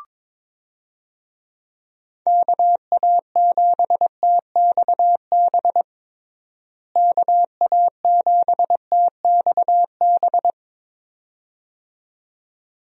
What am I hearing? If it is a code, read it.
Morse code, heard as a keyed tone, about 21 wpm, 700 Hz. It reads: KA7TXB KA7TXB
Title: USA Call Signs - 6 Characters - Repeated 2x - 22wpm